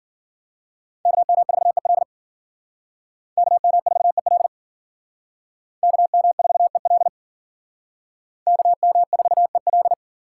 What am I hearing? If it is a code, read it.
Morse code, heard as a keyed tone, about 51 wpm, 700 Hz. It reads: KM4EL KM4EL KM4EL KM4EL